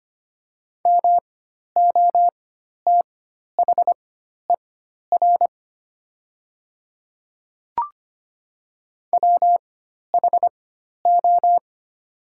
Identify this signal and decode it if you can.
Morse code, heard as a keyed tone, about 25 wpm, spaced as for 12 wpm, 700 Hz. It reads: MOTHER WHO
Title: 4x Character Spacing - Top 200 Words - 25wpm